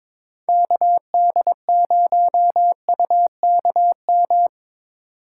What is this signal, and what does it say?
Morse code, heard as a keyed tone, about 24 wpm, 700 Hz. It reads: KD0UKM
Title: Speed Racing - USA 2x3 Call Signs 22wpm